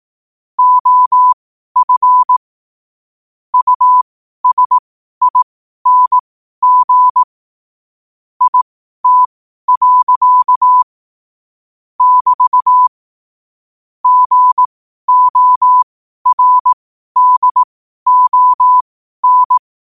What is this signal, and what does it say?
Morse code, heard as a keyed tone, about 18 wpm, spaced as for 12 wpm, 1000 Hz. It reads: OF USING IT. <BT> GORDON
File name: qotdfiles / 12wpm